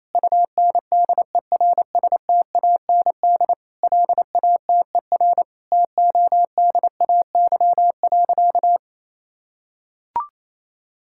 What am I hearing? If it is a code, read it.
Morse code, heard as a keyed tone, about 28 wpm, 700 Hz. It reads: UNDERSTAND LATER TODAY.
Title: Sentences from Top 1000 Words - Review Part 5 - 28wpm